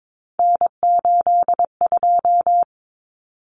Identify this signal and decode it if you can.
Morse code, heard as a keyed tone, about 22 wpm, 700 Hz. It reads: N82
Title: Common Cut Numbers - 3-Digit 22wpm